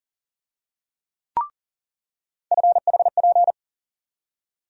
Morse code, heard as a keyed tone, about 40 wpm, 700 Hz. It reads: FHP
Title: USA Call Sign Postfixes - 3 Letters 40wpm